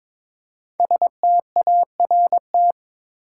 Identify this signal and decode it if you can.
Morse code, heard as a keyed tone, about 22 wpm, 700 Hz. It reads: START
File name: 5 Letter Words - First 3 Letters Given - 22wpm